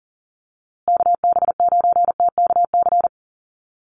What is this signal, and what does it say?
Morse code, heard as a keyed tone, about 40 wpm, 700 Hz. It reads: KB9TKC
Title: USA Call Signs Repeated 2x 40wpm